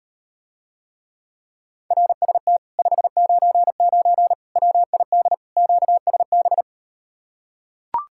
Morse code, heard as a keyed tone, about 39 wpm, 700 Hz. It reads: RST 599 WID QSB